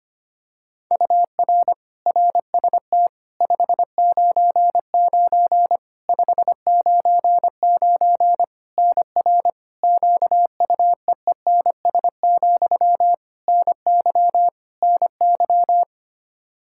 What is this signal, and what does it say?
Morse code, heard as a keyed tone, about 25 wpm, 700 Hz. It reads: UR RST 599 599 NR QUEENS, NY NY
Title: K1USN SST QSO Practice 25wpm